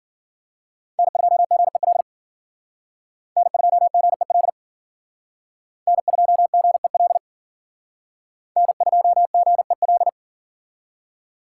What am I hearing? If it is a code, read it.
Morse code, heard as a keyed tone, about 52 wpm, 700 Hz. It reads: N2GEL N2GEL N2GEL N2GEL